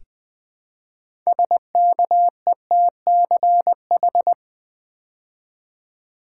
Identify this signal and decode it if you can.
Morse code, heard as a keyed tone, about 20 wpm, 700 Hz. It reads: SKETCH
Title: Build A Word - 3 Letters At A Time - 6-Letter Words - 20wpm